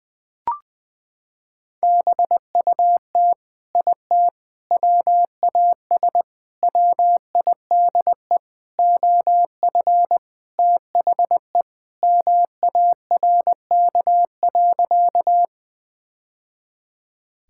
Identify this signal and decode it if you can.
Morse code, heard as a keyed tone, about 20 wpm, 700 Hz. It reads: BUT IT WAS WIDE OF THE MARK.
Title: Sentences from Top 1000 Words - Review Part 6 - 20wpm